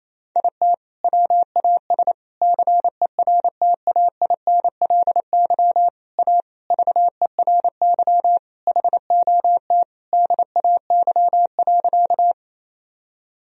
Morse code, heard as a keyed tone, about 27 wpm, 700 Hz. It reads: IT WAS CERTAINLY A VERY HOT DAY.
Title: Sentences from Top 1000 Words - Review Part 3 - 28wpm